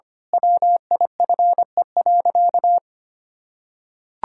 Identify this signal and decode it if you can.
Morse code, heard as a keyed tone, about 25 wpm, 700 Hz. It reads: WIFE.
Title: Pangram Challenge v2 - 25wpm